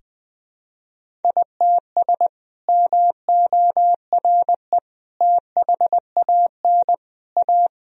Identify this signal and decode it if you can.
Morse code, heard as a keyed tone, about 20 wpm, 700 Hz. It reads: ITS MORE THAN A
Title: Sentences from Top 1000 Words - Review Part 1 - 20wpm